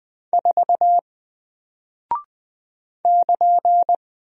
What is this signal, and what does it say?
Morse code, heard as a keyed tone, about 20 wpm, 700 Hz. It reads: 4 <KN>
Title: Single Character (Letter:Number:Punctuation:Prosign) Alt 20wpm